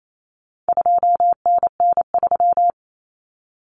Morse code, heard as a keyed tone, about 28 wpm, 700 Hz. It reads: 2NN3
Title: Common Cut Numbers - 4-Digit 28wpm